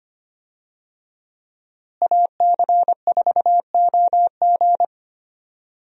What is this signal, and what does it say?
Morse code, heard as a keyed tone, about 25 wpm, 700 Hz. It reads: AC4OG
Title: USA Call Signs - 5 Characters - Repeated 3x - 25wpm